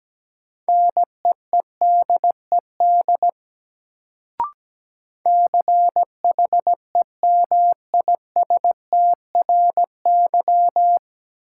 Morse code, heard as a keyed tone, about 17 wpm, 700 Hz. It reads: NEEDED CHEMISTRY
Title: Sentences from Top 1000 Words - Review Part 6 - New Word Drill - 17wpm